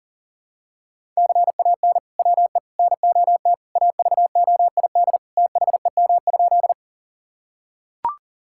Morse code, heard as a keyed tone, about 40 wpm, 700 Hz. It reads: CAN WE NOT AVOID THEM?
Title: Sentences from Top 1000 Words - Review Part 6 - 40wpm